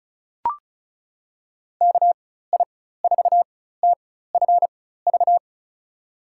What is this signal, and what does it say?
Morse code, heard as a keyed tone, about 36 wpm, spaced as for 16 wpm, 700 Hz. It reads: KI4TFV
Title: USA Call Signs - 4x Character Spacing - 35wpm